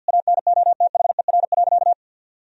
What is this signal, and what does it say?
Morse code, heard as a keyed tone, about 51 wpm, 700 Hz. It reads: ANOTHER.
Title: Sentences from Top 1000 Words - Review Part 1 - 50wpm